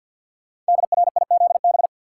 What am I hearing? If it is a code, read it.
Morse code, heard as a keyed tone, about 48 wpm, 690 Hz. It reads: DRIZB
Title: Code Groups - 5 Letters - 50wpm